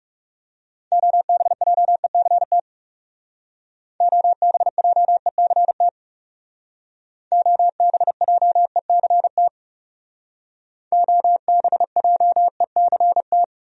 Morse code, heard as a keyed tone, about 35 wpm, 700 Hz. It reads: OBJECT OBJECT OBJECT OBJECT
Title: Speed-Racing Top 500 Words 30wpm